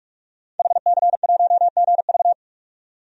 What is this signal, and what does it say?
Morse code, heard as a keyed tone, about 42 wpm, 700 Hz. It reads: HC1GV
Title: DXCC-100 Call Signs - 5 Characters - 45wpm